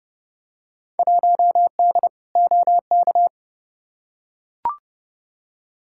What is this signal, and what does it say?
Morse code, heard as a keyed tone, about 31 wpm, 700 Hz. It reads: 1D OK
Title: Field Day Exchanges - 30wpm